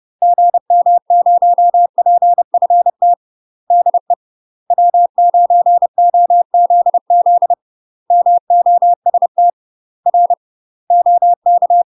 Morse code, heard as a keyed tone, about 30 wpm, 700 Hz. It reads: GM0PFT DE W9OZZ MOST R OK
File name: UK MASTER QSOs TRAINING / 30WPM / UK MASTER QSOs 0030